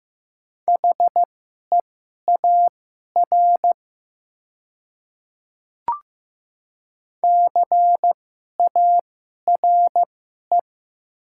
Morse code, heard as a keyed tone, about 15 wpm, spaced as for 11 wpm, 700 Hz. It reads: HEAR CARE
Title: 2x Character Spacing - Top 500 Words - 15wpm